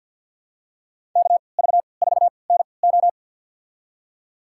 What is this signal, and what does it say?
Morse code, heard as a keyed tone, about 47 wpm, spaced as for 28 wpm, 700 Hz. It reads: KV4NO
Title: USA Call Signs - 5 Characters - 3x Character Spacing - 50wpm